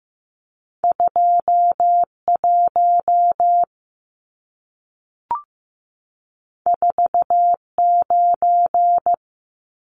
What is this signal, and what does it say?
Morse code, heard as a keyed tone, about 15 wpm, 700 Hz. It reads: 21 49
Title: Numbers - 2-Digit 15wpm